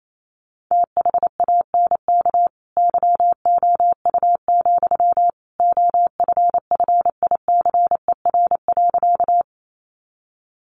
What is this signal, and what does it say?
Morse code, heard as a keyed tone, about 28 wpm, 700 Hz. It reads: THANK YOU, OFFICER.
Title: Sentences from Top 1200 Words 28wpm